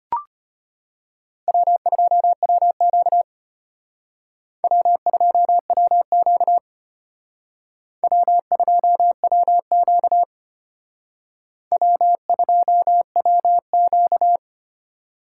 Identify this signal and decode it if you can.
Morse code, heard as a keyed tone, about 32 wpm, 700 Hz. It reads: W2WQ W2WQ W2WQ W2WQ